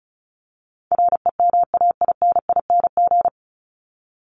Morse code, heard as a keyed tone, about 34 wpm, 700 Hz. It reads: REMAINING
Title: ING Suffix - Top 100 Words with Base Words 35wpm